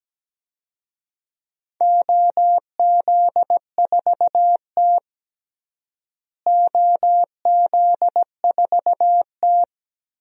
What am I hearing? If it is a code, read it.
Morse code, heard as a keyed tone, about 17 wpm, 700 Hz. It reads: OZ4T OZ4T
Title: DXCC-100 Call Signs - 4 Characters - Repeated 2x - 17wpm